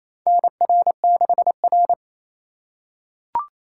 Morse code, heard as a keyed tone, about 28 wpm, 700 Hz. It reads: NR6R
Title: USA Call Signs - 4 Characters - 3x Character Spacing - 28wpm